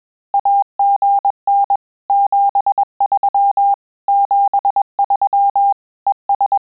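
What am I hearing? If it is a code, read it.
Morse code, heard as a keyed tone, about 21 wpm, 800 Hz. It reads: AGN 73 73 ES
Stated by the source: Morse QSO practice at 20 words per minute, in Italian and English.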